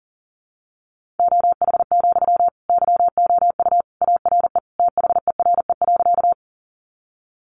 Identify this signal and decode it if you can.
Morse code, heard as a keyed tone, about 40 wpm, 700 Hz. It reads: OH, YOU ARE THERE.
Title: Sentences from Top 500 Words 40wpm